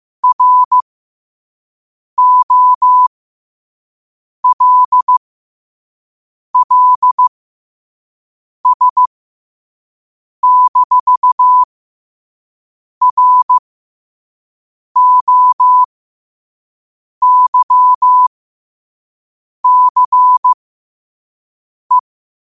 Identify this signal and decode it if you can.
Morse code, heard as a keyed tone, about 15 wpm, spaced as for 5 wpm, 1000 Hz. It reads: ROLLS-ROYCE
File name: qotdfiles / 5wpm